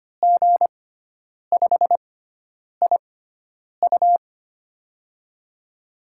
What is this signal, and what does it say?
Morse code, heard as a keyed tone, about 25 wpm, spaced as for 9 wpm, 700 Hz. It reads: G5IU
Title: USA Call Signs - 5 Characters - 6x Character Spacing - 25wpm